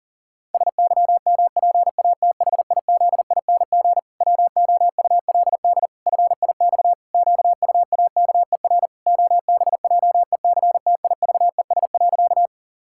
Morse code, heard as a keyed tone, about 40 wpm, 700 Hz. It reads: SYMPATHIZING WOULD FIX QUAKER OBJECTIVES.
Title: Pangram Challenge v2 - 40wpm